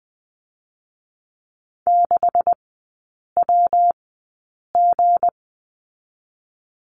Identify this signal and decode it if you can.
Morse code, heard as a keyed tone, about 20 wpm, 700 Hz. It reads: 6 W G
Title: 198 Lesson 32e - Letter k - Three Characters 20wpm